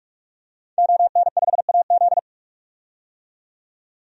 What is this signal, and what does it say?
Morse code, heard as a keyed tone, about 44 wpm, 700 Hz. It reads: ON5AZ
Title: DXCC-100 Call Signs - 5 Characters - 45wpm